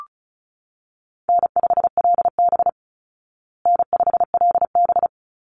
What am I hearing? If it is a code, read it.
Morse code, heard as a keyed tone, about 34 wpm, 700 Hz. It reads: N5LB N5LB
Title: Call Signs Repeated 3x - 35wpm